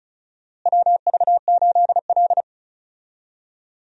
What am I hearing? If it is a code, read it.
Morse code, heard as a keyed tone, about 34 wpm, 700 Hz. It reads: WV8L
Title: USA Call Signs - 4 Characters - Repeated 2x - 35wpm